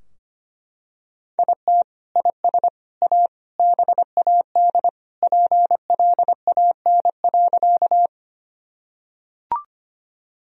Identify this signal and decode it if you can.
Morse code, heard as a keyed tone, about 25 wpm, 700 Hz. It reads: IT IS A BAD PLAN.
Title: Sentences from Top 1000 Words - Review Part 2 - 25wpm